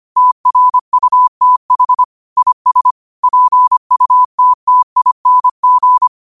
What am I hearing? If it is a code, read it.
Morse code, heard as a keyed tone, about 25 wpm, 1000 Hz. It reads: TRUTH IS PUTTING
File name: qotdfiles / 25wpm